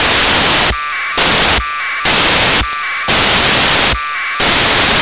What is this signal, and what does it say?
Packet radio (AX.25, as in APRS).